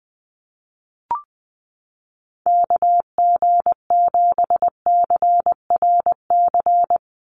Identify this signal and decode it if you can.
Morse code, heard as a keyed tone, about 20 wpm, 700 Hz. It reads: KG7CRC
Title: USA Call Signs 20wpm